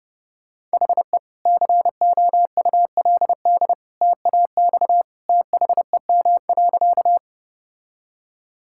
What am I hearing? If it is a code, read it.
Morse code, heard as a keyed tone, about 30 wpm, 700 Hz. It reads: HE COULD TAX THEM.
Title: Sentences from Top 1000 Words - Review Part 5 - 30wpm